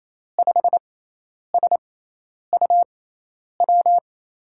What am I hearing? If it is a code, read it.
Morse code, heard as a keyed tone, about 28 wpm, spaced as for 10 wpm, 700 Hz. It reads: 5SUW
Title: USA Call Signs - 6 Characters - 6x Character Spacing - 28wpm